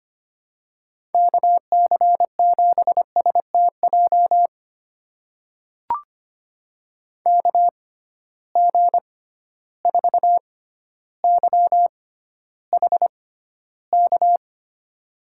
Morse code, heard as a keyed tone, about 25 wpm, spaced as for 13 wpm, 700 Hz. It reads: KC7STJ KG4YHK
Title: USA Call Signs - 6 Characters - 6x Character Spacing - 25wpm